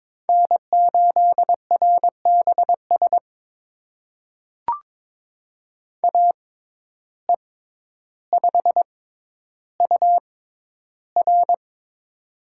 Morse code, heard as a keyed tone, about 22 wpm, spaced as for 11 wpm, 700 Hz. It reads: N8RBS AE5UR
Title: USA Call Signs - 5 Characters - 6x Character Spacing - 22wpm